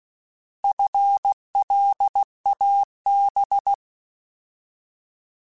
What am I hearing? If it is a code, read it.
Morse code, heard as a keyed tone, about 16 wpm, 800 Hz. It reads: FLAB
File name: Alphabetical / Alphabetical mp3s / Fs @15